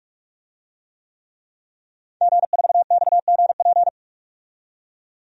Morse code, heard as a keyed tone, about 46 wpm, 700 Hz. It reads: G4XGP
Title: DXCC-100 Call Signs - 5 Characters - 45wpm